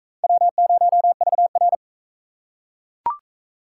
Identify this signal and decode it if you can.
Morse code, heard as a keyed tone, about 43 wpm, 700 Hz. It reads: W0VR